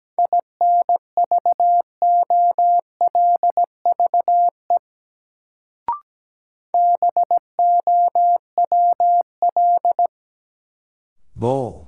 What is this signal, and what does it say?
Morse code, heard as a keyed tone, about 17 wpm, 700 Hz. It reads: INVOLVE BOWL
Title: Sentences from Top 1200 Words - New Word Drill Alt - 17wpm